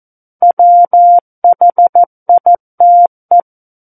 Morse code, heard as a keyed tone, about 14 wpm, 700 Hz. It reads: WHITE
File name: Book One / 05-Chapter 5 - 14 WPM